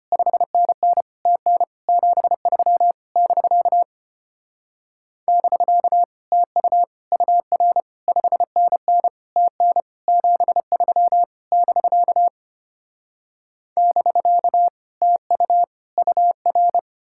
Morse code, heard as a keyed tone, about 33 wpm, 700 Hz. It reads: 5NN TN 73 <BK> <BK> TU UR 5NN TN 73 <BK> <BK> TU UR